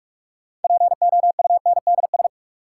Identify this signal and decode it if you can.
Morse code, heard as a keyed tone, about 42 wpm, 700 Hz. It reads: POUNDS
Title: Sentences from Top 1000 Words - Review Part 2 - New Word Drill - 45wpm